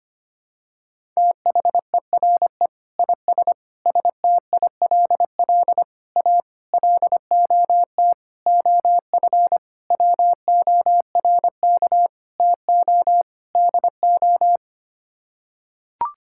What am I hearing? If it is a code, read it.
Morse code, heard as a keyed tone, about 25 wpm, 700 Hz. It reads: THERE IS STILL A LOT OF WORK TO DO
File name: Sets of 9 Words - 25wpm